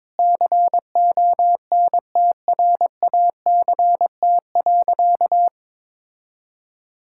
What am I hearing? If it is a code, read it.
Morse code, heard as a keyed tone, about 22 wpm, 700 Hz. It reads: CONTRACT.
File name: Sentences with 9-Word Phrases - 22wpm